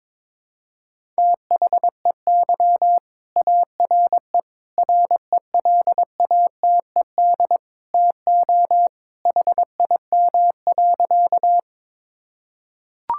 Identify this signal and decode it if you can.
Morse code, heard as a keyed tone, about 22 wpm, 700 Hz. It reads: THEY ARE RELATED TO HIM.
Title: Sentences from Top 1000 Words - Review Part 5 - 22wpm